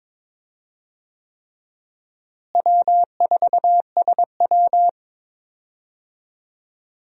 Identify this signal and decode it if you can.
Morse code, heard as a keyed tone, about 21 wpm, 700 Hz. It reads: W4SW
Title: USA Call Signs - 4 Characters - Repeated 3x - 22wpm